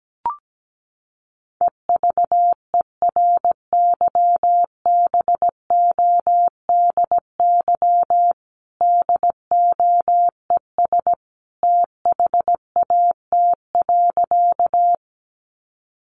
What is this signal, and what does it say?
Morse code, heard as a keyed tone, about 17 wpm, 700 Hz. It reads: EVERYBODY DOES THAT.
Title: Sentences from Top 1400 Words 17wpm